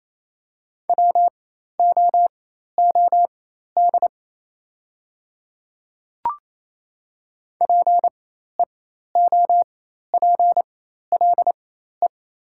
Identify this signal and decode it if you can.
Morse code, heard as a keyed tone, about 28 wpm, spaced as for 13 wpm, 690 Hz. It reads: WOOD PEOPLE
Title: 4x Character Spacing - Top 300 Words - 28wpm